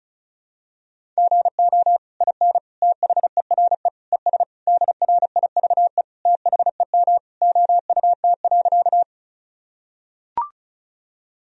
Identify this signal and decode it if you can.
Morse code, heard as a keyed tone, about 36 wpm, 700 Hz. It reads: GO IN THERE ES DRIVE THEM OUT.
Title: Sentences from Top 500 Words and Abbreviations 35wpm